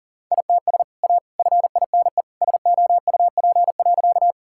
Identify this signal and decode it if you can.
Morse code, heard as a keyed tone, about 40 wpm, 700 Hz. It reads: ITS A FINE SOUP.
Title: Sentences from Top 1000 Words - Review Part 6 - 40wpm